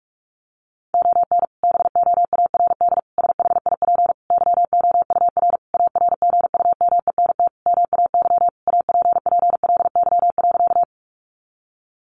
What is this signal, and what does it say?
Morse code, heard as a keyed tone, about 44 wpm, 700 Hz. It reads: ON BOARD SHIP YOUR ARGUMENT MAY APPLY.
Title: Sentences from Top 1400 Words 45wpm